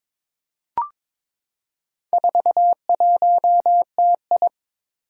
Morse code, heard as a keyed tone, about 23 wpm, 700 Hz. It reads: 41TI
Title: Four Challenging Letter-Number Combinations 22wpm